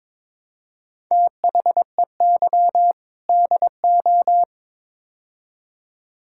Morse code, heard as a keyed tone, about 22 wpm, 700 Hz. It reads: THEY DO
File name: practice / 22wpm / Build A Phrase - 5-Word Phrases - 22wpm